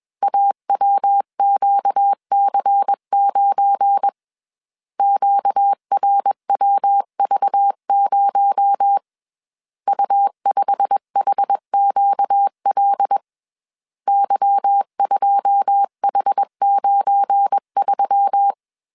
Morse code, heard as a keyed tone, about 21 wpm, 800 Hz. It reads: AWQC9 QRWV0 U5HQL Y2H93